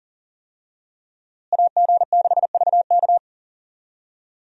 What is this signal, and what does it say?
Morse code, heard as a keyed tone, about 37 wpm, 700 Hz. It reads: AG6VK
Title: USA Call Signs - 5 Characters - Repeated 3x - 40wpm